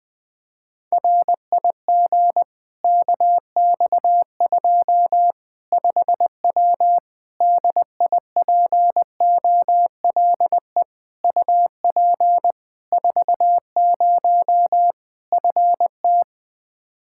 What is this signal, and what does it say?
Morse code, heard as a keyed tone, about 20 wpm, 700 Hz. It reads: RIG KX2 5W DIPOLE UP 40 FT
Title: Practice QSO - Exchange 2 - Info - 20wpm